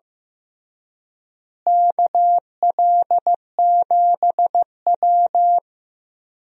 Morse code, heard as a keyed tone, about 15 wpm, 700 Hz. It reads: KL7W
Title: DXCC-100 Call Signs - 4 Characters - Repeated 3x - 15wpm